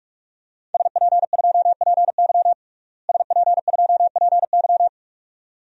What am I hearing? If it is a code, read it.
Morse code, heard as a keyed tone, about 42 wpm, 700 Hz. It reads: SP2PY SP2PY
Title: DXCC-100 Call Signs - 5 Characters - Repeated 2x - 45wpm